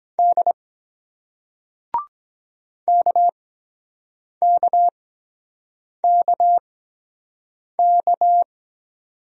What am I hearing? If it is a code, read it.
Morse code, heard as a keyed tone, about 23 wpm, spaced as for 7 wpm, 700 Hz. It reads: D KKKK